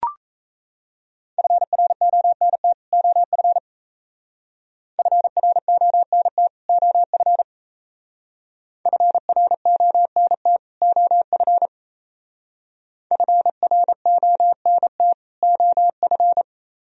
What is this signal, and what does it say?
Morse code, heard as a keyed tone, about 34 wpm, 700 Hz. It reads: FRONT OF FRONT OF FRONT OF FRONT OF
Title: Speed-Racing - Sets of 2 Words 28wpm